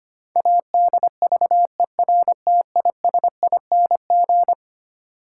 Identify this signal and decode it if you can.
Morse code, heard as a keyed tone, about 25 wpm, 700 Hz. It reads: ADVERTISING
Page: morse-code-ninja-practice-25wpm